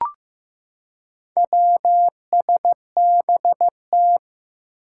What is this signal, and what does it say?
Morse code, heard as a keyed tone, about 15 wpm, 700 Hz. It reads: WSBT
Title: Four Challenging Letter-Number Combinations 15wpm